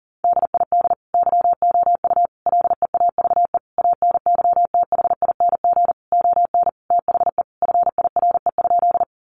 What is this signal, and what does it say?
Morse code, heard as a keyed tone, about 40 wpm, 700 Hz. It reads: DID YOU LEAVE ANYTHING ON THE FIRE?
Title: Sentences from Top 1000 Words 40wpm